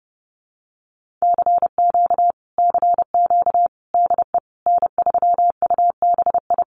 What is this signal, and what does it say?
Morse code, heard as a keyed tone, about 30 wpm, 700 Hz. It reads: CQ CQ DE N3UBI
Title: Common QSO Phrases Encore 30wpm